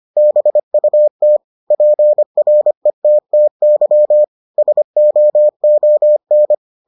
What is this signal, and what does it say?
Morse code, heard as a keyed tone, about 25 wpm, 600 Hz. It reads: BUT PRETTY SOON